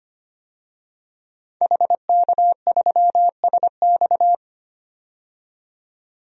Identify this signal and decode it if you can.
Morse code, heard as a keyed tone, about 25 wpm, 700 Hz. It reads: HK3SX
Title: DXCC-100 Call Signs - 5 Characters - 25wpm